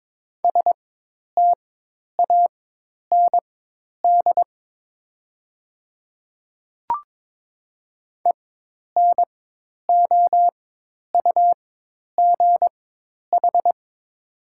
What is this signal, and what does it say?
Morse code, heard as a keyed tone, about 22 wpm, spaced as for 10 wpm, 700 Hz. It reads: STAND ENOUGH
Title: 4x Character Spacing - Top 500 Words - 22wpm